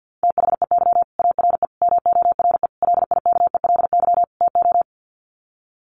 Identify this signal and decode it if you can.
Morse code, heard as a keyed tone, about 50 wpm, 700 Hz. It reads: THEY ARE MORE LIKELY TO
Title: Sets of 5 Words Encore2 50wpm